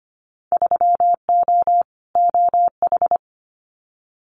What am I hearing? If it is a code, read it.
Morse code, heard as a keyed tone, about 25 wpm, 700 Hz. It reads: 3O OH
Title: Winter Field Day Exchanges 25wpm